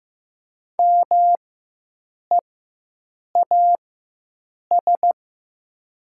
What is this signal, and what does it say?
Morse code, heard as a keyed tone, about 15 wpm, spaced as for 7 wpm, 700 Hz. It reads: MEAS
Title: 4x Character Spacing - Top 500 Words - 15wpm